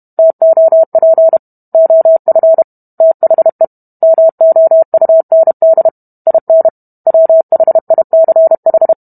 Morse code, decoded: TOP OF THE MOUND IN WHICH

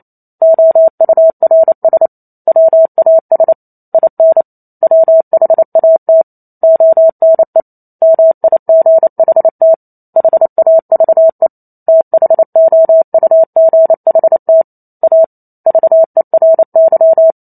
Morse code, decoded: OURS WAS IN WHAT ONE MIGHT HAVE THOUGHT A VERY

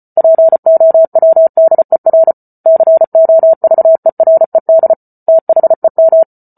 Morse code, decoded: POWDER COVERED THEM

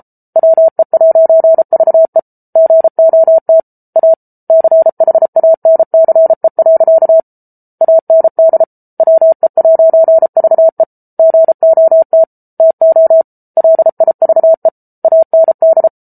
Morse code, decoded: WE'VE GOT A CHANCE. AND WE'VE GOT TO LIVE AND

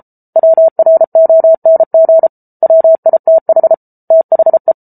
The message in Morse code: WRONG WITH THE

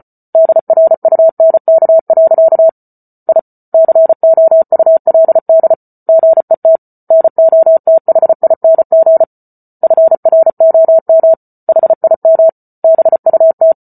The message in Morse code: DRUNK. I COULD GET NOTHING FROM HIM BUT

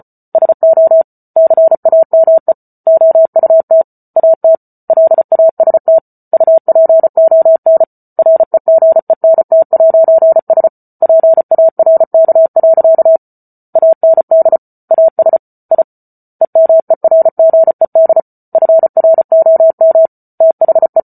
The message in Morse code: SO CAME OUT AT LAST UPON REGENT'S PARK. AND AS I EMERGED FROM THE

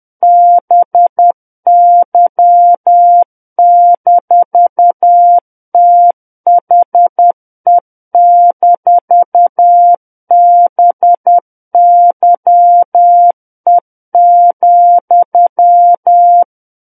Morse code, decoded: BY-THE-BYE,

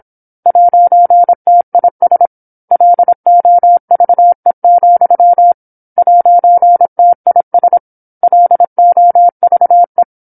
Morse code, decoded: 'TIS LOVE, 'TIS LOVE

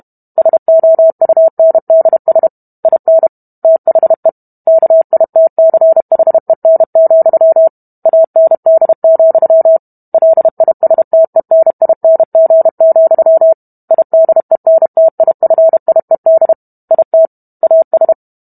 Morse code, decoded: SOUNDS IN THE KITCHEN, AND, LISTENING, IDENTIFIED IT AS

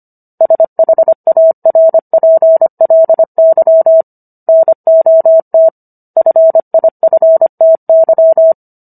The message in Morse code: SHARPLY NOT FIFTY

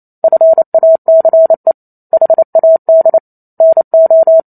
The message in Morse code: FACE HAD NO

